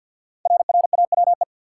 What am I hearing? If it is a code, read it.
Morse code, heard as a keyed tone, about 48 wpm, 700 Hz. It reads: RRAWE
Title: Code Groups - 5 Letters - 50wpm